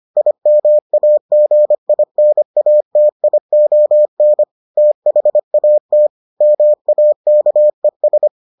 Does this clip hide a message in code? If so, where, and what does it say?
Yes: Morse code from 0.2 to 8.3 s, heard as a keyed tone, about 25 wpm, 600 Hz. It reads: IMAGINATION THAT MAKES